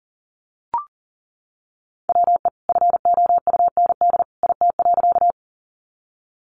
Morse code, heard as a keyed tone, about 40 wpm, 700 Hz. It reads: WE FOUND IT.
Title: Sentences from Top 300 Words 40wpm